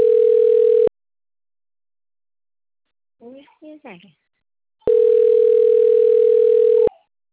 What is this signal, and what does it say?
Ringback tone (North America), 440 + 480 Hz, with speech.
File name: rg-1600-unknown- - - .52929